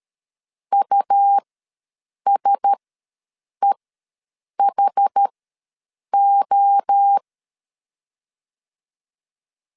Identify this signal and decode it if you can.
Morse code, heard as a keyed tone, about 13 wpm, spaced as for 7 wpm, 800 Hz. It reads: USEHO
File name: LEZIONI / 5 LBVC / 5 LBV ws 12wpm 3015